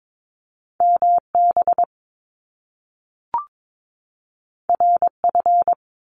Morse code, heard as a keyed tone, about 22 wpm, 700 Hz. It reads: MB RF
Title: Top Two Letter Combinations in Words 22wpm